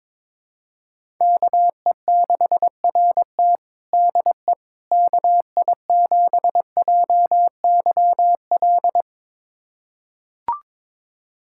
Morse code, heard as a keyed tone, about 22 wpm, 700 Hz. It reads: KE6RT DE KI7JYL
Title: USA Callsign DE Callsign 22wpm